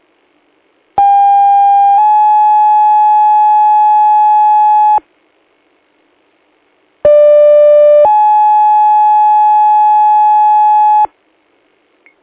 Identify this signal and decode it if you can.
Two-tone page, heard as a one-second tone, then a three-second tone, about 800 then 850 Hz, about 590 then 850 Hz. It is heard from 1.0 to 11.1 s.